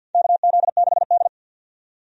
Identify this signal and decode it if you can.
Morse code, heard as a keyed tone, about 52 wpm, 700 Hz. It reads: KZ6D
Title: USA Call Signs - 4 Characters - 50wpm